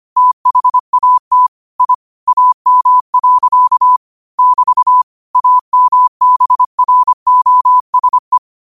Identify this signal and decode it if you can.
Morse code, heard as a keyed tone, about 25 wpm, 1000 Hz. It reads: THAT I AM. <BT> AMBROSE
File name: qotdfiles / 25wpm